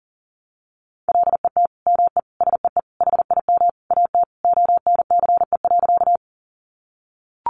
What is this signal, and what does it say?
Morse code, heard as a keyed tone, about 40 wpm, 700 Hz. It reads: LET ME SEE HIM AT ONCE.
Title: Sentences from Top 300 Words 40wpm